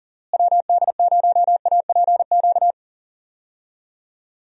Morse code, heard as a keyed tone, about 40 wpm, 700 Hz. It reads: WD0APQ
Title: USA Call Signs - 6 Characters - Repeated 3x - 40wpm